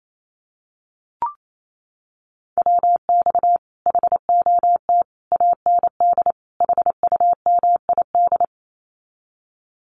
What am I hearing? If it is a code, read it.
Morse code, heard as a keyed tone, about 28 wpm, 700 Hz. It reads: WX HOT AND HUMID
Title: Common QSO Phrases Encore 28wpm